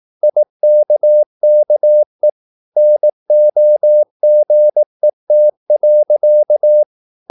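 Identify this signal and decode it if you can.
Morse code, heard as a keyed tone, about 18 wpm, 600 Hz. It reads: IKKE NOGET.
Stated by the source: Slow (QRS) Morse practice at 18 wpm, CW med Gnister.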